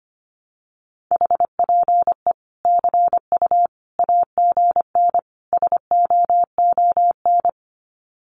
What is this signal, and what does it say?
Morse code, heard as a keyed tone, about 25 wpm, 700 Hz. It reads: HPE CU AGN SOON
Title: Common QSO Phrases 25wpm